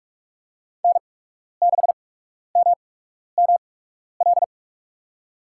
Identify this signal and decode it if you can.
Morse code, heard as a keyed tone, about 47 wpm, 700 Hz. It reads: N 6 M M L
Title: USA Call Signs - 5 Characters - 8x Character Spacing - 45wpm